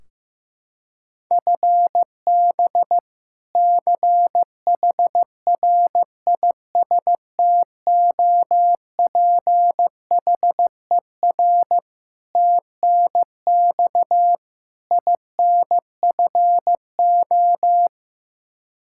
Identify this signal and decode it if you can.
Morse code, heard as a keyed tone, about 15 wpm, 700 Hz. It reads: FB CHRISTOPHER TNX INFO
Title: K1USN SST QSO Practice 15wpm